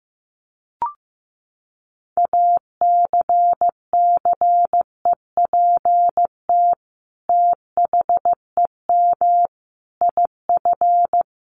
Morse code, heard as a keyed tone, about 15 wpm, 700 Hz. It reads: ACCEPT THEM IF
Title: Sentences from Top 1300 Words 15wpm